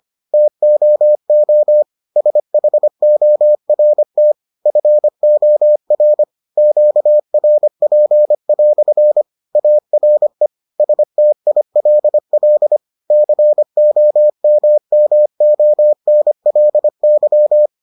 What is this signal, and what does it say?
Morse code, heard as a keyed tone, about 25 wpm, 600 Hz. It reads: TOO SHORT FOR QRP" ARE STILL COMMONLY